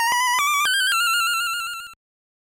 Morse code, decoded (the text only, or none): none